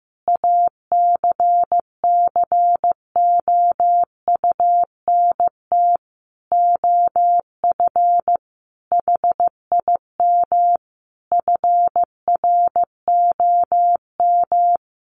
Morse code, heard as a keyed tone, about 15 wpm, 700 Hz. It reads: ACCOUNT OF HIM FROM
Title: Sentences from Top 900 Words 15wpm